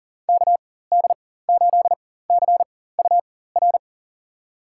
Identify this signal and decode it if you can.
Morse code, heard as a keyed tone, about 38 wpm, spaced as for 18 wpm, 700 Hz. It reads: KD8CUR
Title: USA Call Signs - 6 Characters - 4x Character Spacing - 40wpm